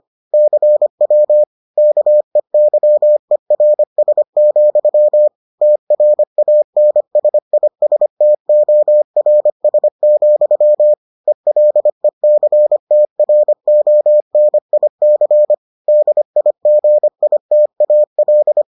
Morse code, heard as a keyed tone, about 25 wpm, 600 Hz. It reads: CW KEYERS, TRANSISTORS, ELECTRONIC DIGITAL